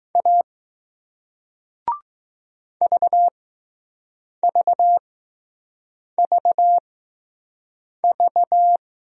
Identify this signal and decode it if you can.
Morse code, heard as a keyed tone, about 21 wpm, spaced as for 6 wpm, 700 Hz. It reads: A VVVV